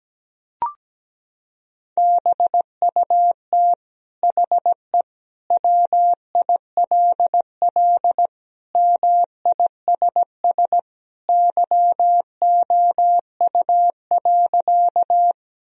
Morse code, heard as a keyed tone, about 17 wpm, 700 Hz. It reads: BUT HE WILL MISS YOU.
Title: Sentences from Top 1000 Words - Review Part 3 - 17wpm